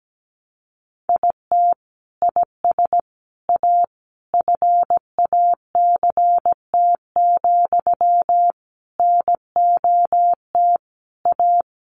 Morse code, heard as a keyed tone, about 17 wpm, 700 Hz. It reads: IT IS A FACT, NOT A
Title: Sentences from Top 1400 Words 17wpm